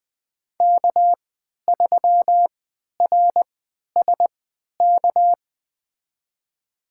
Morse code, heard as a keyed tone, about 20 wpm, spaced as for 11 wpm, 700 Hz. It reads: K3RSK
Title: USA Call Signs - 5 Characters - 3x Character Spacing - 20wpm